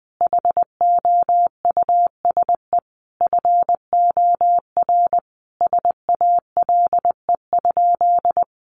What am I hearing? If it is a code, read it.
Morse code, heard as a keyed tone, about 20 wpm, 700 Hz. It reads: HOUSE FOR SALE?
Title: Sentences from Top 1400 Words 20wpm